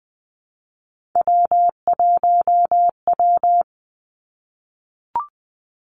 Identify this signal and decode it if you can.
Morse code, heard as a keyed tone, about 19 wpm, 700 Hz. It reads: W1W
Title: Three Challenging Letter-Number Combinations 20wpm